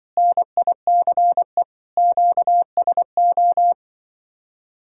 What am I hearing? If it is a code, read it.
Morse code, heard as a keyed tone, about 23 wpm, 700 Hz. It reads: NICE QSO
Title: Speed-Racing Common QSO Phrases - 20wpm